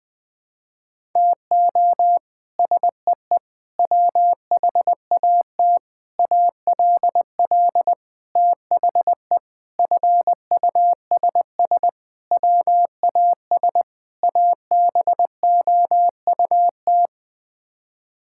Morse code, heard as a keyed tone, about 20 wpm, 700 Hz. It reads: TO SEE WHAT ALL THE FUSS WAS ABOUT
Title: Sets of 8 Words - 20wpm